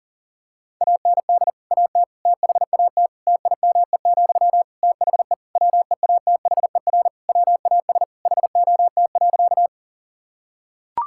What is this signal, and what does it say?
Morse code, heard as a keyed tone, about 37 wpm, 700 Hz. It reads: AND AT THAT TIME, THE WEATHER WAS HOT.
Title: Sentences from Top 1000 Words - Review Part 5 - 40wpm